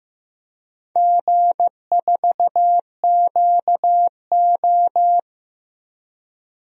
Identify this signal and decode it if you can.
Morse code, heard as a keyed tone, about 15 wpm, 700 Hz. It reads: G4QO
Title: DXCC-100 Call Signs - 4 Characters - Repeated 3x - 15wpm